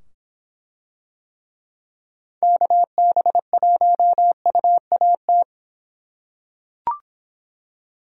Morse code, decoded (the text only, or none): KB1UAT